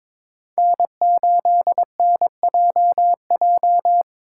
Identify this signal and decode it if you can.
Morse code, heard as a keyed tone, about 26 wpm, 700 Hz. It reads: N8NJJ
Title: Speed Racing - USA 1x3 Call Signs 22wpm